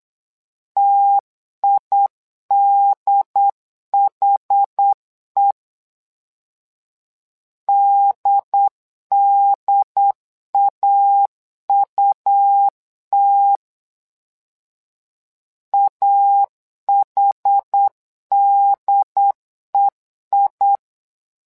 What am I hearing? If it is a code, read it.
Morse code, heard as a keyed tone, about 8 wpm, 800 Hz. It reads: TIDHE DDAUT AHDEI
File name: LEZIONI / 2 UHDN / 2 UHD 8wpm 2258